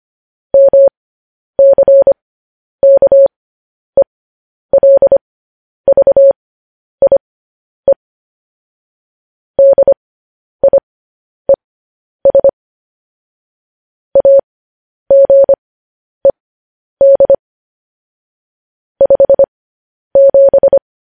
Morse code, heard as a keyed tone, about 25 wpm, spaced as for 10 wpm, 550 Hz. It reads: MCKELVIE DIES AGED 57
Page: news-headlines-in-morse-code-at-10-wpm